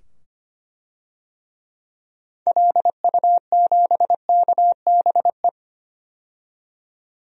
Morse code, heard as a keyed tone, about 25 wpm, 700 Hz. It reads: LU7KBE
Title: DXCC-100 Call Signs - 6 Characters - 25wpm